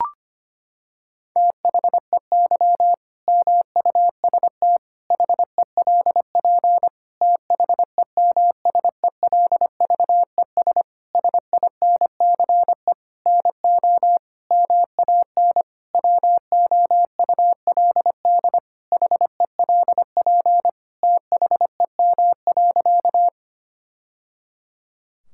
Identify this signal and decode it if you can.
Morse code, heard as a keyed tone, about 25 wpm, 700 Hz. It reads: THEY MUST HELP THEMSELVES SINCE NO MAN WOULD HELP THEM.
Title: Sentences from Top 1000 Words - Review Part 5 - 25wpm